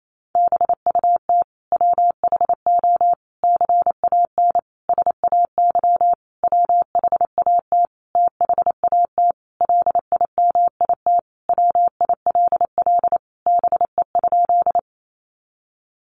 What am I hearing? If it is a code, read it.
Morse code, heard as a keyed tone, about 28 wpm, 700 Hz. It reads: BUT WHO CAN SAY WHAT THAT LIMIT WILL BE?
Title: Sentences from Top 1500 Words 28wpm